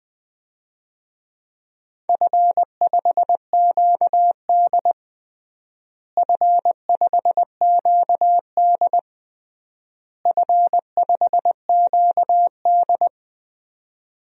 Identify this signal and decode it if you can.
Morse code, heard as a keyed tone, about 20 wpm, 700 Hz. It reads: F5QD F5QD F5QD
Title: DXCC-100 Call Signs - 4 Characters - Repeated 3x - 20wpm